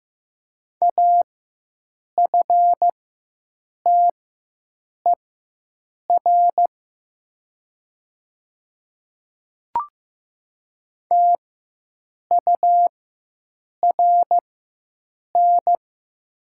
Morse code, heard as a keyed tone, about 15 wpm, spaced as for 7 wpm, 700 Hz. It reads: AFTER TURN
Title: 4x Character Spacing - Top 300 Words - 15wpm